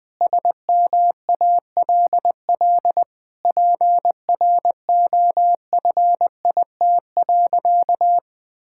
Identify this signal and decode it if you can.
Morse code, heard as a keyed tone, about 20 wpm, 700 Hz. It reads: SMALL PROFIT.
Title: Sentences from Top 1000 Words - Review Part 5 - 20wpm